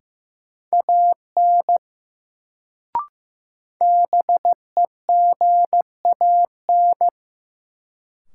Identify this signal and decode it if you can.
Morse code, heard as a keyed tone, about 15 wpm, 700 Hz. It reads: AN BEGAN
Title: Sentences from Top 300 Words - New Word Drill - 15wpm